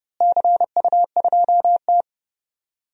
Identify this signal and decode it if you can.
Morse code, heard as a keyed tone, about 30 wpm, 700 Hz. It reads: CU2T
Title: DXCC-100 Call Signs - 4 Characters - Repeated 3x - 30wpm